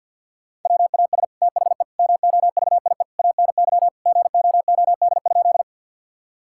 Morse code, heard as a keyed tone, about 50 wpm, 700 Hz. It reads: WAS THE MOVIE ANY GOOD?
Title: Sentences from Top 1000 Words - Review Part 5 - 50wpm